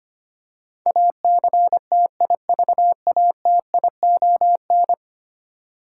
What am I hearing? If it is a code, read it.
Morse code, heard as a keyed tone, about 25 wpm, 700 Hz. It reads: ACTIVATION
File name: 10 Letter Words - First 3 Letters Given - 25wpm